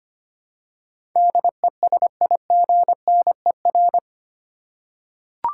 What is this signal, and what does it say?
Morse code, heard as a keyed tone, about 25 wpm, 700 Hz. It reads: DESIGNER
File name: Eight Letter Words - 25wpm